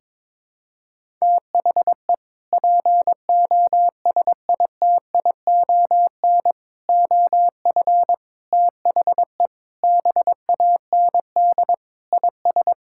Morse code, heard as a keyed tone, about 22 wpm, 700 Hz. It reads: THE POSITION OF THE BAND IS
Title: Sentences from Top 1600 Words 22wpm